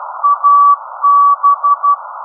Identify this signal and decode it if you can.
Morse code, heard as a keyed tone, about 12 wpm, 1190 Hz. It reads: AB
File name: OUFTI-1 z ON4EZJ